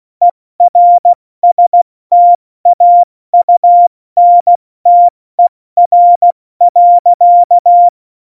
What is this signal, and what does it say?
Morse code, heard as a keyed tone, about 16 wpm, 700 Hz. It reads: ERSTAUNTER.
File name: Morsekurs / Witze / Witz 06. Oktober 2020, 70 F80